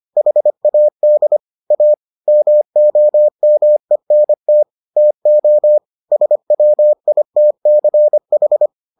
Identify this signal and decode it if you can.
Morse code, heard as a keyed tone, about 25 wpm, 600 Hz. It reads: HAD A MOMENT TO SWITCH